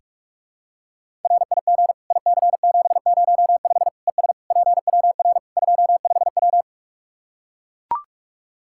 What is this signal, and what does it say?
Morse code, heard as a keyed tone, about 41 wpm, 700 Hz. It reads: RIG IC705 ES PWR 25W
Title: Practice QSO - Exchange 2 - Info - 45wpm